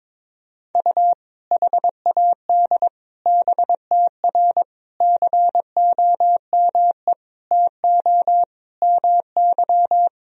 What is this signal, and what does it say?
Morse code, heard as a keyed tone, about 22 wpm, 700 Hz. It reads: U HAD BTR COME TO MY
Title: Sentences from Top 400 Words and Abbreviations 22wpm